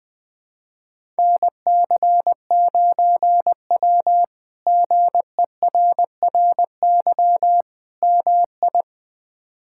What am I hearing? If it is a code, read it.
Morse code, heard as a keyed tone, about 20 wpm, 700 Hz. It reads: NC9W GERRY MI
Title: K1USN SST QSO Practice - 20wpm